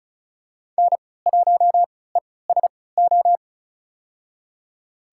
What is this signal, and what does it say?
Morse code, heard as a keyed tone, about 36 wpm, spaced as for 20 wpm, 700 Hz. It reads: N1ESO
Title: USA Call Signs - 5 Characters - 3x Character Spacing - 35wpm